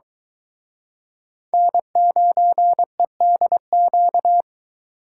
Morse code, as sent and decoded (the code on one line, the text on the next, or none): -. ----. -.. --.-
N9EDQ